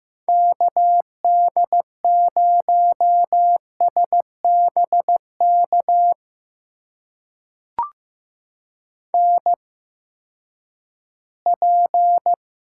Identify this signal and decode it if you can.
Morse code, heard as a keyed tone, about 15 wpm, spaced as for 11 wpm, 700 Hz. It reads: KD0SBK NP
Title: USA Call Signs - 8x Character Spacing - 15wpm